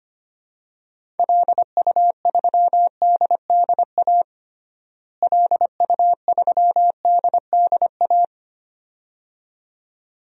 Morse code, heard as a keyed tone, about 25 wpm, 700 Hz. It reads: LU3DDA LU3DDA
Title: DXCC-100 Call Signs - 6 Characters - Repeated 2x - 25wpm